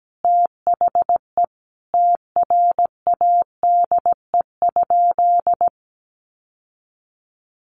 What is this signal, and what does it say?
Morse code, heard as a keyed tone, about 17 wpm, 700 Hz. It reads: THE TRADE?
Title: Sentences from Top 600 Words 17wpm